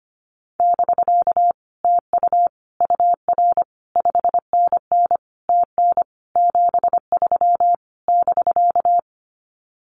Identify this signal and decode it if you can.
Morse code, heard as a keyed tone, about 25 wpm, 700 Hz. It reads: <BK> TU UR 5NN TN 73 <BK>
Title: Common QSO Phrases Encore 25wpm